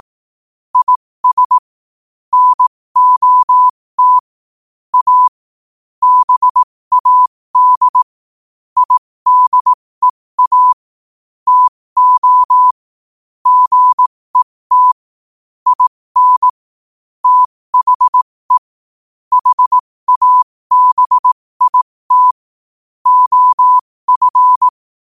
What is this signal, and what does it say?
Morse code, heard as a keyed tone, about 18 wpm, spaced as for 15 wpm, 1000 Hz. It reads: IS NOT A BAD IDEA TO GET IN THE HABIT OF
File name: qotdfiles / 15wpm